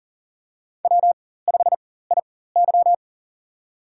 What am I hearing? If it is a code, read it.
Morse code, heard as a keyed tone, about 40 wpm, spaced as for 19 wpm, 700 Hz. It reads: W5IY
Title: USA Call Signs - 4x Character Spacing - 40wpm